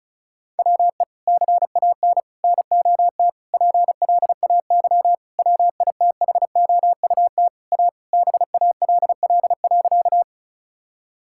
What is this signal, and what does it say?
Morse code, heard as a keyed tone, about 34 wpm, 700 Hz. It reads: WE CAN NOT PLAY WITHOUT A BALL.
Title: Sentences from Top 1000 Words - Review Part 6 - 35wpm